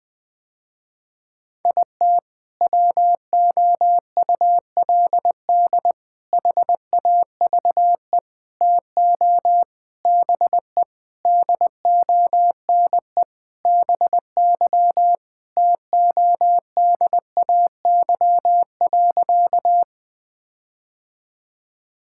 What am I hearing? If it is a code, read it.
Morse code, heard as a keyed tone, about 20 wpm, 700 Hz. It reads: IT WOULD HAVE TO BE DONE BY TODAY.
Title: Sentences with 4-Word Phrases - Spoken Beginning - 20wpm